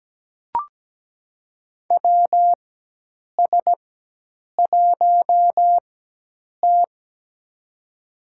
Morse code, heard as a keyed tone, about 17 wpm, spaced as for 8 wpm, 700 Hz. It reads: WS1T
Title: USA Call Signs - 4 Characters - 4x Character Spacing - 17wpm